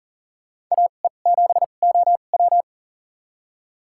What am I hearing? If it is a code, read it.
Morse code, heard as a keyed tone, about 40 wpm, spaced as for 29 wpm, 700 Hz. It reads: AE7OW
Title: USA Call Signs - 5 Characters - 2x Character Spacing - 40wpm